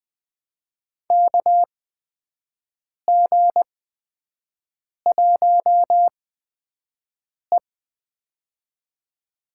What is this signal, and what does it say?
Morse code, heard as a keyed tone, about 21 wpm, spaced as for 5 wpm, 700 Hz. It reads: KG1E
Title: USA Call Signs - 4 Characters - 8x Character Spacing - 20wpm